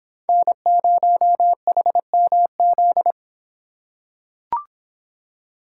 Morse code, heard as a keyed tone, about 25 wpm, 700 Hz. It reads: N0HMZ